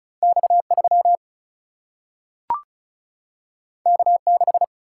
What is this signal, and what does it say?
Morse code, heard as a keyed tone, about 36 wpm, 700 Hz. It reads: X 3 K 6
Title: Single-Letter Single-Number 35wpm